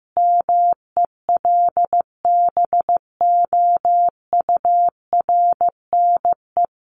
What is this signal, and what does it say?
Morse code, heard as a keyed tone, about 15 wpm, 700 Hz. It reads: MELBOURNE
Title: Common QSO Phrases 15wpm